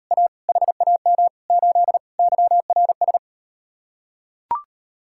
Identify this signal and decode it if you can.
Morse code, heard as a keyed tone, about 38 wpm, 700 Hz. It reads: A HAM 8 YRS